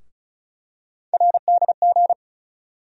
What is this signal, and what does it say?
Morse code, heard as a keyed tone, about 35 wpm, 700 Hz. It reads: RDG
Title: USA Call Sign Postfixes - 3 Letters 35wpm